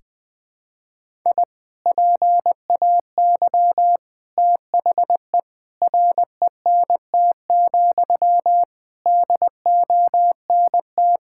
Morse code, heard as a keyed tone, about 21 wpm, 700 Hz. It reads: I PAY THE RENT, DONT
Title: Sentences from Top 1000 Words - Review Part 5 - 20wpm